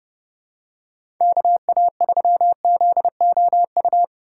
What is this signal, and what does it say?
Morse code, heard as a keyed tone, about 30 wpm, 700 Hz. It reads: KA3ZOU
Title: USA Call Signs - 6 Characters - 6x Character Spacing - 30wpm